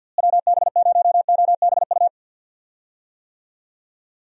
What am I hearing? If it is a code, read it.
Morse code, heard as a keyed tone, about 51 wpm, 700 Hz. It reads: WB0OBU
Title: USA Call Signs - 6 Characters - Repeated 3x - 50wpm